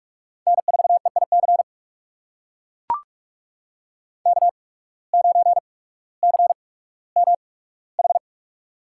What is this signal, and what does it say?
Morse code, heard as a keyed tone, about 44 wpm, spaced as for 19 wpm, 700 Hz. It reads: N4EIC K9CMH
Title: USA Call Signs - 8x Character Spacing - 45wpm